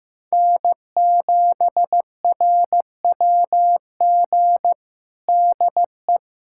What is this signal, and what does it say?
Morse code, heard as a keyed tone, about 15 wpm, 700 Hz. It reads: N7RWG DE
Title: USA Callsign DE Callsign - 5 Characters - 15wpm